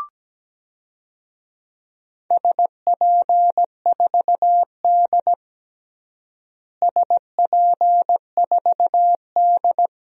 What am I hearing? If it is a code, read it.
Morse code, heard as a keyed tone, about 17 wpm, 700 Hz. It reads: SP4D SP4D
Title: DXCC-100 Call Signs - 4 Characters - Repeated 3x - 17wpm